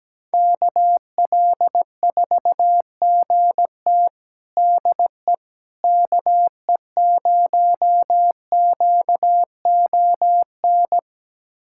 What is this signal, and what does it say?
Morse code, heard as a keyed tone, about 17 wpm, 700 Hz. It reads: KL4GT DE KE0QON
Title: USA Callsign DE Callsign 17wpm